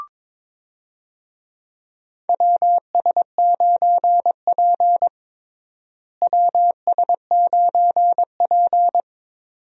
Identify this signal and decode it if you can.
Morse code, heard as a keyed tone, about 22 wpm, 700 Hz. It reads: WS9P WS9P
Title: USA Call Signs - 4 Characters - Repeated 2x - 22wpm